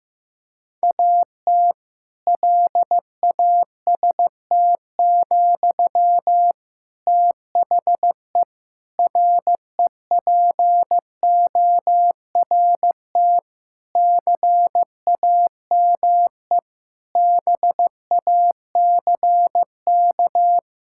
Morse code, heard as a keyed tone, about 15 wpm, 700 Hz. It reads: AT LAST, THE REPORT CAME BACK
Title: Sentences from Top 1000 Words - Review Part 6 - 15wpm